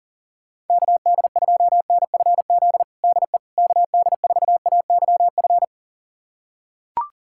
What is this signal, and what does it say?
Morse code, heard as a keyed tone, about 40 wpm, 700 Hz. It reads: KD2NFZ DE KD4AYF
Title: USA Callsign DE Callsign - 6 Characters - 40wpm